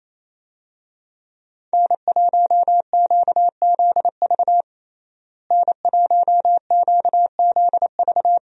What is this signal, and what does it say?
Morse code, heard as a keyed tone, about 28 wpm, 700 Hz. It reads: N1QZV N1QZV
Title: USA Call Signs - 5 Characters - Repeated 2x - 28wpm